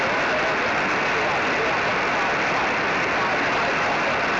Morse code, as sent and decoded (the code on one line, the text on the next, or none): none